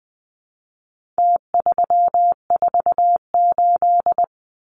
Morse code, heard as a keyed tone, about 20 wpm, 700 Hz. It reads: T348
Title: Numbers - Contest Style - 4-Digit 20wpm